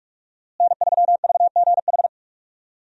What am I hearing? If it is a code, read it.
Morse code, heard as a keyed tone, about 46 wpm, 700 Hz. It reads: N3VGH
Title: USA Call Signs - 5 Characters - Repeated 3x - 45wpm